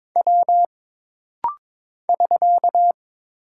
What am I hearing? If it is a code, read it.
Morse code, heard as a keyed tone, about 22 wpm, 700 Hz. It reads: W <SK>
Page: This rapid-fire timing is geared to push your ICR (Instant Character Recognition) skills.